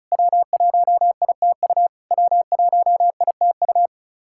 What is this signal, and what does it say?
Morse code, heard as a keyed tone, about 36 wpm, 700 Hz. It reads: W1ITU W1ITU
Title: SOTA QSO Practice 35wpm